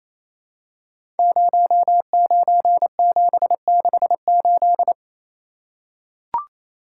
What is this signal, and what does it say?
Morse code, heard as a keyed tone, about 29 wpm, 700 Hz. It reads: 09768
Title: Code Groups - 5 Letter:Numbers - 28wpm